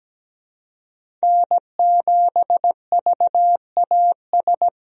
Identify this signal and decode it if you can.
Morse code, heard as a keyed tone, about 17 wpm, 700 Hz. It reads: N7VAS
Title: USA Call Signs - 5 Characters - 17wpm